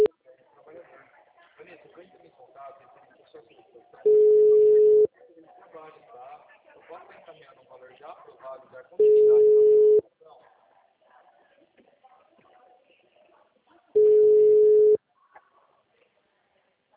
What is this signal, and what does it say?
Ringback tone (Europe, 425 Hz), with speech.